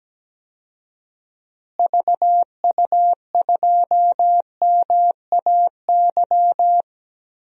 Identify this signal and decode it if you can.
Morse code, heard as a keyed tone, about 17 wpm, 700 Hz. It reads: VU2MAY
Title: DXCC-100 Call Signs - 6 Characters - Repeated 2x - 17wpm